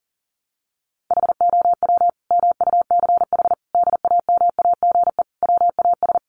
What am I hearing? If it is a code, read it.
Morse code, heard as a keyed tone, about 40 wpm, 700 Hz. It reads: HOW MUCH DAMAGE WAS
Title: Sentences from Top 1100 Words 40wpm